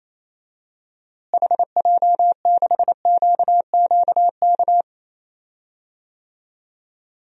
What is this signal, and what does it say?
Morse code, heard as a keyed tone, about 28 wpm, 700 Hz. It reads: HJ6QQK
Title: DXCC-100 Call Signs - 6 Characters - Repeated 3x - 28wpm